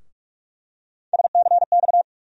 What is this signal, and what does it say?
Morse code, heard as a keyed tone, about 43 wpm, 700 Hz. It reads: SCX
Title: USA Call Sign Postfixes - 3 Letters 45wpm